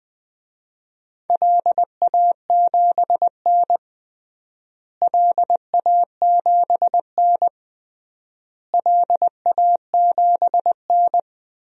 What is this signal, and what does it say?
Morse code, heard as a keyed tone, about 19 wpm, 700 Hz. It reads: LA7N LA7N LA7N
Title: DXCC-100 Call Signs - 4 Characters - Repeated 3x - 20wpm